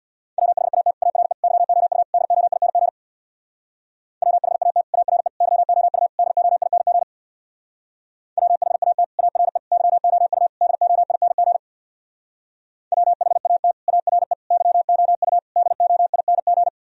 Morse code, heard as a keyed tone, about 64 wpm, 700 Hz. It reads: WHAT ARE YOU DOING WHAT ARE YOU DOING WHAT ARE YOU DOING WHAT ARE YOU DOING